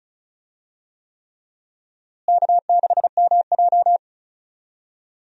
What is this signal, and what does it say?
Morse code, heard as a keyed tone, about 36 wpm, 700 Hz. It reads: K6MJ
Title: USA Call Signs - 4 Characters - Repeated 3x - 35wpm